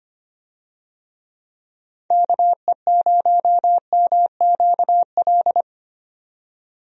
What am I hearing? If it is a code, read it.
Morse code, heard as a keyed tone, about 25 wpm, 700 Hz. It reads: KE0MQL
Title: USA Call Signs - 6 Characters - 25wpm